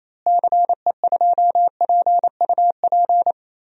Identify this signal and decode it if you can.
Morse code, heard as a keyed tone, about 28 wpm, 700 Hz. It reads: CE2PUP
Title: DXCC-100 Call Signs - 6 Characters - Repeated 3x - 28wpm